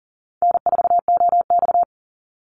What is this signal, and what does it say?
Morse code, heard as a keyed tone, about 40 wpm, 700 Hz. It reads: N4OX
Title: Easy Call Signs Repeated 3x - 40wpm